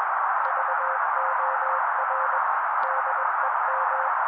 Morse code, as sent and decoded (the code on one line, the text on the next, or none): ...- --- .-. / -.. --
VOR DEM